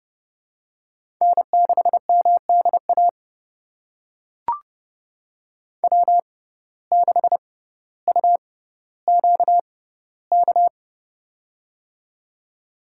Morse code, heard as a keyed tone, about 30 wpm, spaced as for 15 wpm, 700 Hz. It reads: N6MDA W6UQK
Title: USA Call Signs - 5 Characters - 6x Character Spacing - 30wpm